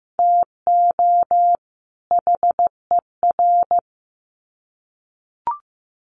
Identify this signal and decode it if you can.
Morse code, heard as a keyed tone, about 15 wpm, 700 Hz. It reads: TO HER
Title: Sets of 2 Words 15wpm